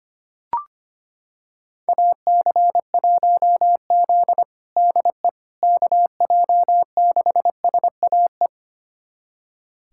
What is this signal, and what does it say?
Morse code, heard as a keyed tone, about 25 wpm, 700 Hz. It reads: AC1Z DE KJ6SAE
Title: USA Callsign DE Callsign 25wpm